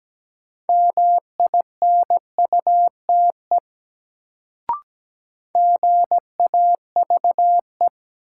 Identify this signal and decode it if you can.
Morse code, heard as a keyed tone, about 17 wpm, 700 Hz. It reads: MINUTE GAVE
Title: Sentences from Top 500 Words - New Word Drill - 17wpm